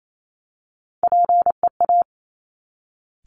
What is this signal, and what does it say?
Morse code, heard as a keyed tone, about 28 wpm, 700 Hz. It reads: PEA
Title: Top Three Letter Combinations in Words 28wpm